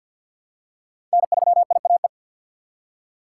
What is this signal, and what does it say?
Morse code, heard as a keyed tone, about 53 wpm, 700 Hz. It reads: N3IAE
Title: USA Call Signs - 5 Characters - 3x Character Spacing - 50wpm